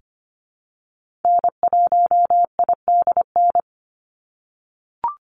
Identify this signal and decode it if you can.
Morse code, heard as a keyed tone, about 25 wpm, 700 Hz. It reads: N1IDN
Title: NA SOTA Activator Callsigns 25wpm